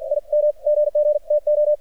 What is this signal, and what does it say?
Morse code, heard as a keyed tone, about 53 wpm, 600 Hz. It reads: I M GG TO
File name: archived / CRSnet